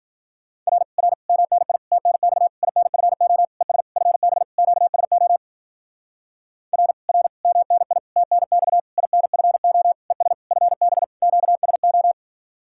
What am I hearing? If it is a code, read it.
Morse code, heard as a keyed tone, about 51 wpm, 700 Hz. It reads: R R MNI TNX INFO ES FB QSO R R MNI TNX INFO ES FB QSO